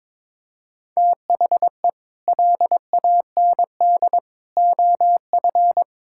Morse code, heard as a keyed tone, about 22 wpm, 700 Hz. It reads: THE LAND OF
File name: Sets of 9 Words - 22wpm